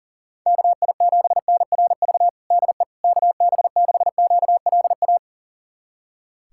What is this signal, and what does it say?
Morse code, heard as a keyed tone, about 37 wpm, 700 Hz. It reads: KI7NRV DE KB6QLA
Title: USA Callsign DE Callsign 40wpm